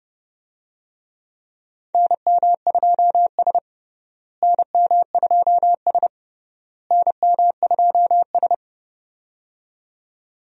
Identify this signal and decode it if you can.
Morse code, heard as a keyed tone, about 30 wpm, 700 Hz. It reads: NM2S NM2S NM2S
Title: USA Call Signs - 4 Characters - Repeated 3x - 30wpm